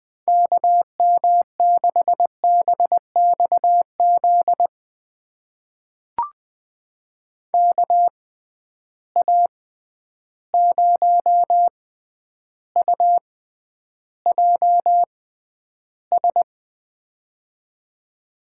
Morse code, heard as a keyed tone, about 20 wpm, spaced as for 10 wpm, 700 Hz. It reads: KM6BXZ KA0UJS
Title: USA Call Signs - 6 Characters - 6x Character Spacing - 20wpm